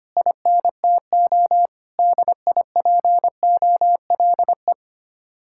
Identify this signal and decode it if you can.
Morse code, heard as a keyed tone, about 25 wpm, 700 Hz. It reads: INTO DIPOLE
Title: Speed-Racing Common QSO Phrases Encore - 25wpm